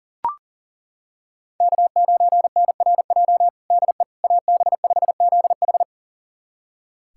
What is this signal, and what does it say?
Morse code, heard as a keyed tone, about 40 wpm, 700 Hz. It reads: K9NRJ DE AB5ZH
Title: USA Callsign DE Callsign - 5 Characters - 40wpm